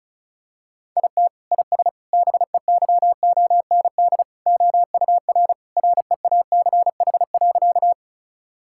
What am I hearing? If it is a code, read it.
Morse code, heard as a keyed tone, about 36 wpm, 700 Hz. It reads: IT IS BEYOND OUR REACH.
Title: Sentences from Top 1000 Words - Review Part 6 - 35wpm